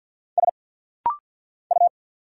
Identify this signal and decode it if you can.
Morse code, heard as a keyed tone, about 50 wpm, 690 Hz. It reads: S U
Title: Single Letter-Number Alt - Rapid-Fire 50wpm